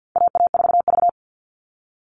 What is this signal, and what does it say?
Morse code, heard as a keyed tone, about 50 wpm, 700 Hz. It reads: AA4V
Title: All Cut Numbers - 4-Digit 50wpm